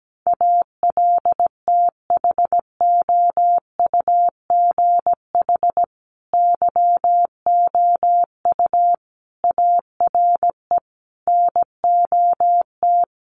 Morse code, heard as a keyed tone, about 17 wpm, 700 Hz. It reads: ALTHOUGH YOU ARE NOT
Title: Sentences from Top 1100 Words 17wpm